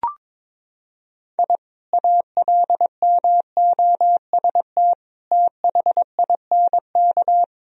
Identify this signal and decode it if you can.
Morse code, heard as a keyed tone, about 22 wpm, 700 Hz. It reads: I ALMOST THINK
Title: Sentences from Top 1000 Words - Review Part 4 - 22wpm